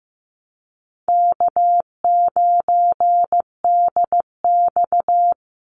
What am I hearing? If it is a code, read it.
Morse code, heard as a keyed tone, about 15 wpm, 700 Hz. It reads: K9DX
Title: Easy Call Signs - 15wpm